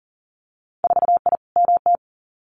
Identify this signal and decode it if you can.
Morse code, heard as a keyed tone, about 40 wpm, 700 Hz. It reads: 4I MT
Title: Winter Field Day Exchanges Encore 40wpm